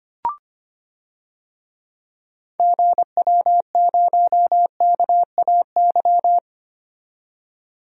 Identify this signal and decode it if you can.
Morse code, heard as a keyed tone, about 25 wpm, 700 Hz. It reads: GW0KAY
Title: DXCC-100 Call Signs - 6 Characters - 25wpm